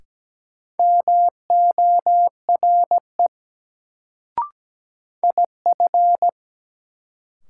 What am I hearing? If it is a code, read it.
Morse code, heard as a keyed tone, about 17 wpm, 700 Hz. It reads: MORE IF
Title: Sentences from Top 100 Words - New Word Drill - 17wpm